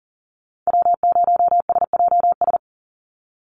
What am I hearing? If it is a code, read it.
Morse code, heard as a keyed tone, about 39 wpm, 700 Hz. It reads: W0SJS
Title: Call Signs - 40wpm